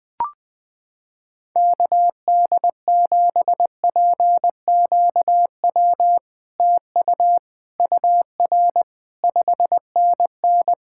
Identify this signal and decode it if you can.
Morse code, heard as a keyed tone, about 21 wpm, 700 Hz. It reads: KD7PQW TU UR 5NN
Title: SOTA QSO Practice 20wpm